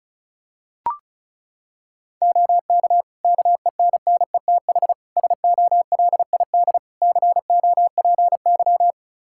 Morse code, heard as a keyed tone, about 34 wpm, 700 Hz. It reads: OK KENNETH SOLID COPY
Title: Practice QSO - Exchange 3 - More Info - 35wpm